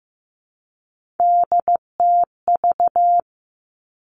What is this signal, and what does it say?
Morse code, heard as a keyed tone, about 15 wpm, 700 Hz. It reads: DTV
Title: Three Challenging Letter-Number Combinations 15wpm